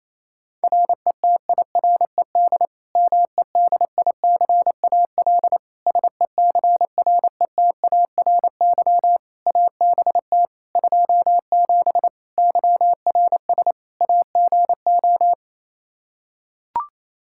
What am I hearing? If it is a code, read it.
Morse code, heard as a keyed tone, about 27 wpm, 700 Hz. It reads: RETIRED MEDICAL SECRETARY ABT 27 YRS AGO
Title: Practice QSO - Exchange 3 - More Info - 28wpm